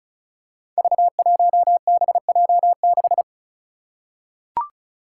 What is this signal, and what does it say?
Morse code, heard as a keyed tone, about 34 wpm, 700 Hz. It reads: V1BJ6
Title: Five Challenging Letter-Number Combinations 35wpm